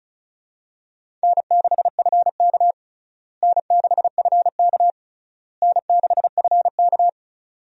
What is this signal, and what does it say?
Morse code, heard as a keyed tone, about 36 wpm, 700 Hz. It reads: N6FK N6FK N6FK
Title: USA Call Signs - 4 Characters - Repeated 3x - 35wpm